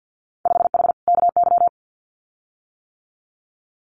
Morse code, heard as a keyed tone, about 50 wpm, 700 Hz. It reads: 5H KY
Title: Winter Field Day Exchanges Encore 50wpm